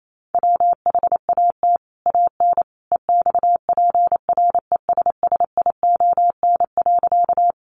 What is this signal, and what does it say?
Morse code, heard as a keyed tone, about 28 wpm, 700 Hz. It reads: WHAT AN EXPRESSION.
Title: Sentences from Top 1200 Words 28wpm